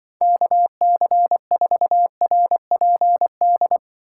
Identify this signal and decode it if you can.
Morse code, heard as a keyed tone, about 24 wpm, 700 Hz. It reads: KC4RPD
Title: Speed Racing - USA 2x3 Call Signs 20wpm